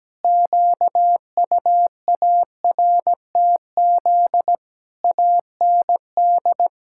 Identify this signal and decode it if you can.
Morse code, heard as a keyed tone, about 17 wpm, 700 Hz. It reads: QUARTZ AND
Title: Pangram Challenge v2 - 17wpm